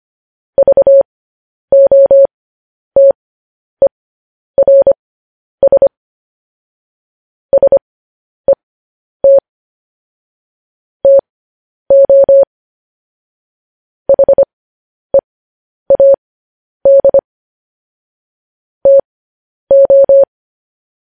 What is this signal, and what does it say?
Morse code, heard as a keyed tone, about 25 wpm, spaced as for 10 wpm, 550 Hz. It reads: VOTERS SET TO HEAD TO
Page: news-headlines-in-morse-code-at-10-wpm